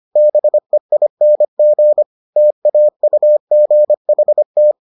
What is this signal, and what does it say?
Morse code, heard as a keyed tone, about 25 wpm, 600 Hz. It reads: BEING TAUGHT